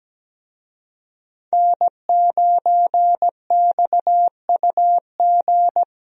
Morse code, heard as a keyed tone, about 17 wpm, 700 Hz. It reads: N9XUG
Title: USA Call Signs - 5 Characters - Repeated 3x - 17wpm